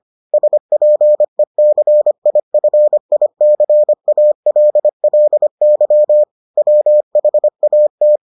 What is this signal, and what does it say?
Morse code, heard as a keyed tone, about 25 wpm, 600 Hz. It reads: SPECIFICALLY WHAT